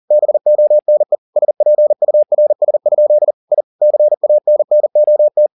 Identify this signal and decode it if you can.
Morse code, heard as a keyed tone, about 40 wpm, 600 Hz. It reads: BONE SPURS? I CANNOT
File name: the old man and the sea 2 40wpm